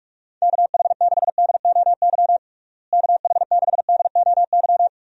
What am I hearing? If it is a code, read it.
Morse code, heard as a keyed tone, about 46 wpm, 700 Hz. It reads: KH6DOY KH6DOY
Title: DXCC-100 Call Signs - 6 Characters - Repeated 2x - 45wpm